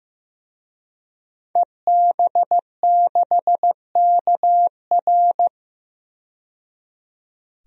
Morse code, heard as a keyed tone, about 16 wpm, 700 Hz. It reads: EB6KR
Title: DXCC-100 Call Signs - 5 Characters - 15wpm